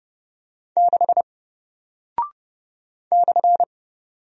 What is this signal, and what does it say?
Morse code, heard as a keyed tone, about 30 wpm, 700 Hz. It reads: 6 /
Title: Single Character (Letter:Number:Punctuation:Prosign) Alt 30wpm